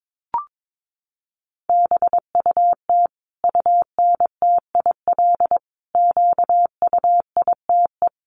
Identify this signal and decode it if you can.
Morse code, heard as a keyed tone, about 22 wpm, 700 Hz. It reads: BUT UNTIL QUITE
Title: Sentences from Top 1200 Words 22wpm